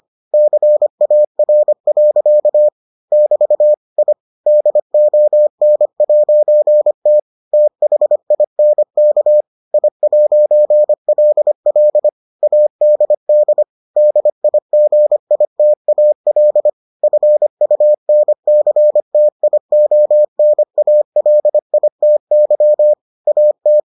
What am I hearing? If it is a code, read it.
Morse code, heard as a keyed tone, about 25 wpm, 600 Hz. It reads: CAR. <BT> I DON'T THINK I'LL ADD DIGITAL FUNCTIONALITY AT